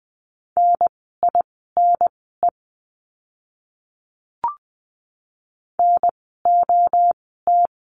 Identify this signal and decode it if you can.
Morse code, heard as a keyed tone, about 20 wpm, spaced as for 15 wpm, 700 Hz. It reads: NINE NOT
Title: 030 Lesson 6f - Letter i - Words 20wpm